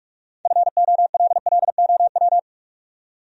Morse code, heard as a keyed tone, about 58 wpm, 700 Hz. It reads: FOLLOW